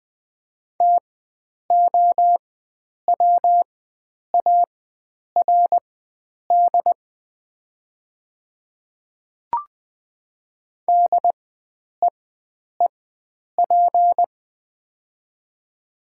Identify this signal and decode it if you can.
Morse code, heard as a keyed tone, about 20 wpm, spaced as for 9 wpm, 700 Hz. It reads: TOWARD DEEP
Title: 4x Character Spacing - Top 500 Words - 20wpm